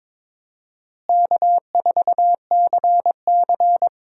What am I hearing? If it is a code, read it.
Morse code, heard as a keyed tone, about 22 wpm, 700 Hz. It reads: K4CC
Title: USA Call Signs - 4 Characters - Repeated 3x - 22wpm